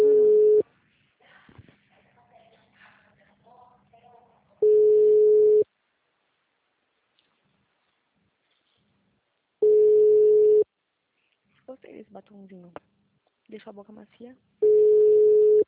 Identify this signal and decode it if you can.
Ringback tone (Europe, 425 Hz), with speech.